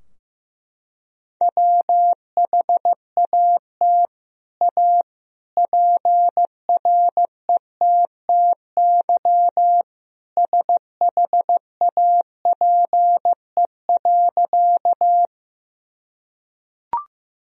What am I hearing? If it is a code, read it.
Morse code, heard as a keyed tone, about 15 wpm, 700 Hz. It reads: WHAT A PRETTY SHAPE.
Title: Sentences from Top 1000 Words - Review Part 3 - 15wpm